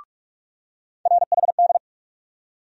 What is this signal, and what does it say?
Morse code, heard as a keyed tone, about 42 wpm, 700 Hz. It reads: RHD
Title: USA Call Sign Postfixes - 3 Letters 45wpm